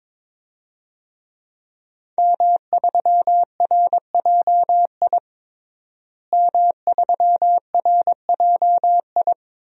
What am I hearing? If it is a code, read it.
Morse code, heard as a keyed tone, about 22 wpm, 700 Hz. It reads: M3RJI M3RJI
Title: DXCC-100 Call Signs - 5 Characters - Repeated 2x - 22wpm